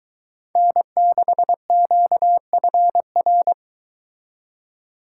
Morse code, heard as a keyed tone, about 23 wpm, 700 Hz. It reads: N6QFR